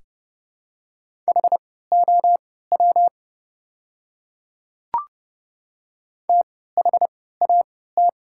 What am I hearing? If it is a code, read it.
Morse code, heard as a keyed tone, about 30 wpm, spaced as for 17 wpm, 690 Hz. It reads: HOW THAT
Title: 3x Character Spacing - Top 100 Words - 30wpm